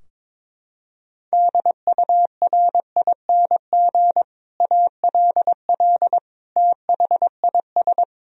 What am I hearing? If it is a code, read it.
Morse code, heard as a keyed tone, about 22 wpm, 700 Hz. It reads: DURING ALL THIS
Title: Sentences from Top 1000 Words - Review Part 3 - 22wpm